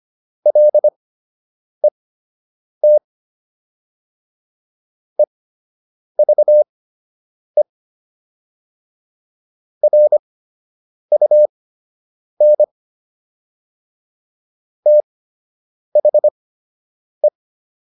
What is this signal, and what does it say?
Morse code, heard as a keyed tone, about 25 wpm, spaced as for 8 wpm, 600 Hz. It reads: LET EVE RUN THE